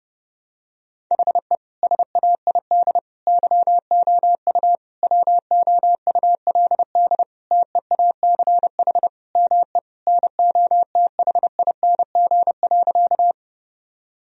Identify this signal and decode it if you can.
Morse code, heard as a keyed tone, about 30 wpm, 700 Hz. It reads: HE SAID YOU WOULD TEACH ME NOTHING.
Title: Sentences from Top 1000 Words - Review Part 2 - 30wpm